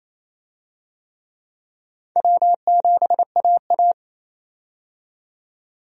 Morse code, heard as a keyed tone, about 28 wpm, 700 Hz. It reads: W7AA
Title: USA Call Signs - 4 Characters - 28wpm